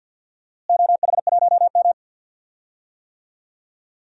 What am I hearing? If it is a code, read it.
Morse code, heard as a keyed tone, about 47 wpm, 700 Hz. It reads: OH1M
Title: DXCC-100 Call Signs - 4 Characters - Repeated 3x - 50wpm